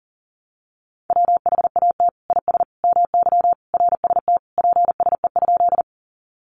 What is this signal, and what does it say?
Morse code, heard as a keyed tone, about 40 wpm, 700 Hz. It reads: WHAT IS MY RST PSE?
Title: Common QSO Phrases 40wpm